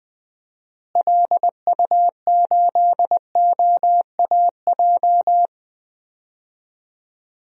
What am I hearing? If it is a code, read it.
Morse code, heard as a keyed tone, about 21 wpm, 700 Hz. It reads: LU8OAJ
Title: DXCC-100 Call Signs - 6 Characters - Repeated 2x - 20wpm